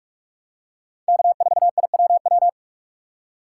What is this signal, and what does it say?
Morse code, heard as a keyed tone, about 46 wpm, 700 Hz. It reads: K4IWW
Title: USA Call Signs - 5 Characters - 3x Character Spacing - 45wpm